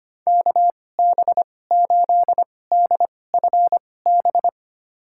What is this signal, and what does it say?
Morse code, heard as a keyed tone, about 25 wpm, spaced as for 18 wpm, 700 Hz. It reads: KB8DFB
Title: USA Call Signs - 2x Character Spacing - 25wpm